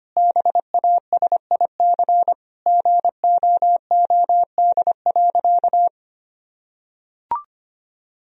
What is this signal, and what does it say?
Morse code, heard as a keyed tone, about 25 wpm, 700 Hz. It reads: BASIC GOOD.
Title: Sentences from Top 1000 Words - Review Part 3 - 25wpm